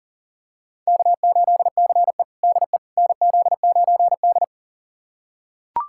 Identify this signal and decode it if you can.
Morse code, heard as a keyed tone, about 40 wpm, 700 Hz. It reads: K8CE DE NZ9D
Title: USA Callsign DE Callsign - 4 Characters - 40wpm